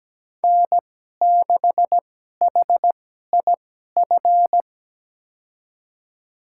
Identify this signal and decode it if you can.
Morse code, heard as a keyed tone, about 17 wpm, spaced as for 12 wpm, 700 Hz. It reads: N6HIF
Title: USA Call Signs - 5 Characters - 2x Character Spacing - 17wpm